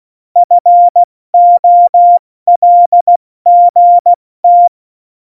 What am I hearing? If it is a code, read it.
Morse code, heard as a keyed tone, about 16 wpm, 700 Hz. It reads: FOLGT
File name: Morsekurs / Witze / Witz 29. Januar 2019, 70 F80